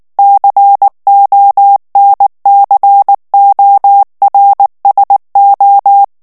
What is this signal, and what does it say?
Morse code, heard as a keyed tone, about 19 wpm, 800 Hz. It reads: CONCORSO
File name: morse mp3 / ITALIEN texte clair 2 vitesse 18 mots minute